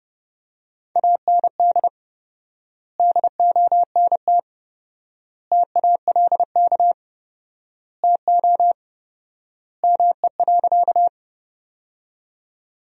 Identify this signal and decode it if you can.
Morse code, heard as a keyed tone, about 30 wpm, 700 Hz. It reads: AND DONT TALK TO ME.
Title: 4x Word-Spacing - Sentences from Top 400 Words - 30wpm